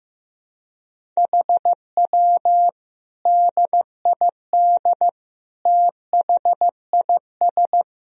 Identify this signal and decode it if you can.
Morse code, heard as a keyed tone, about 16 wpm, 700 Hz. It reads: HW DID THIS
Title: Sentences from Top 400 Words and Abbreviations 15wpm